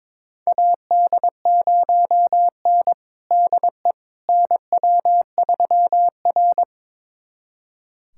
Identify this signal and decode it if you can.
Morse code, heard as a keyed tone, about 22 wpm, 700 Hz. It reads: AD0N DE NW3R
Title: USA Callsign DE Callsign - 4 Characters - 22wpm